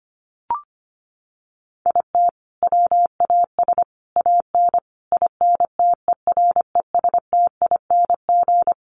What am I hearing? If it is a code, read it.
Morse code, heard as a keyed tone, about 25 wpm, 700 Hz. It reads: IT WAS AN INTERESTING
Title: Sentences from Top 1100 Words 25wpm